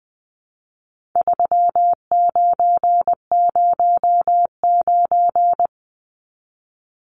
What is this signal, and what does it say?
Morse code, heard as a keyed tone, about 20 wpm, 700 Hz. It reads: 3909
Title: Numbers - 4-Digit 20wpm